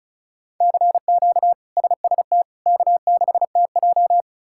Morse code, heard as a keyed tone, about 36 wpm, 700 Hz. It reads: CQ SST K6TJ
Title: K1USN SST QSO Practice - 35wpm